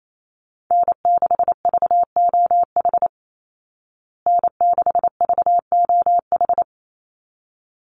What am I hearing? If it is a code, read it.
Morse code, heard as a keyed tone, about 28 wpm, 700 Hz. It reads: N6VOH N6VOH
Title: Call Signs Repeated 3x - 28wpm